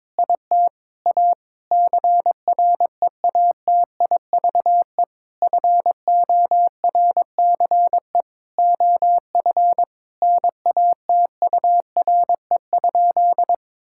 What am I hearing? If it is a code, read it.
Morse code, heard as a keyed tone, about 22 wpm, 700 Hz. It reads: IT A CREATIVE FORCE OF NATURE?
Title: Sentences from Top 1000 Words - Review Part 5 - 22wpm